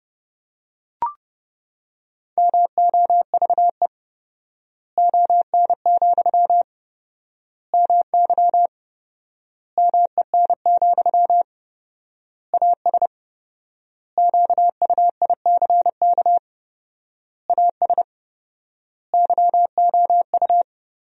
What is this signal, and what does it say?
Morse code, heard as a keyed tone, about 30 wpm, 700 Hz. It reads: MOVE ON, MY MEN, AS QUICK AS YOU
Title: 4x Word-Spacing - Sentences from Top 500 Words - 30wpm